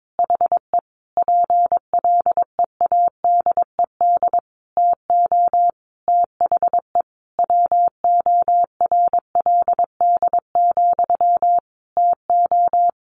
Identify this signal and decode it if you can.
Morse code, heard as a keyed tone, about 22 wpm, 700 Hz. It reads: HE PLEADED TO THE WORLD, TO
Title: A Sound Of Thunder 22wpm